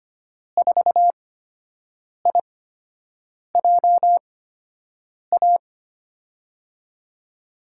Morse code, heard as a keyed tone, about 25 wpm, spaced as for 7 wpm, 700 Hz. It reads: 4IJA
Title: USA Call Signs - 6 Characters - 8x Character Spacing - 25wpm